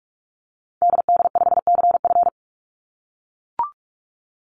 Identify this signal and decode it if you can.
Morse code, heard as a keyed tone, about 44 wpm, 700 Hz. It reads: DD5CF
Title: Call Signs - 45wpm